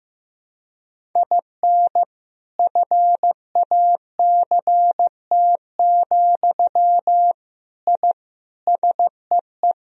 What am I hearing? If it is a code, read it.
Morse code, heard as a keyed tone, about 15 wpm, 700 Hz. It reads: IN FACT, I SEE
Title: Sentences from Top 1000 Words - Review Part 3 - 15wpm